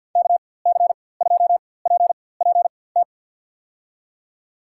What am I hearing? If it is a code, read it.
Morse code, heard as a keyed tone, about 52 wpm, spaced as for 24 wpm, 700 Hz. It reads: KC2PPT
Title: USA Call Signs - 6 Characters - 4x Character Spacing - 50wpm